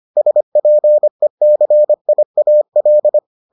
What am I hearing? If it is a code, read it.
Morse code, heard as a keyed tone, about 25 wpm, 600 Hz. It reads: SPECIAL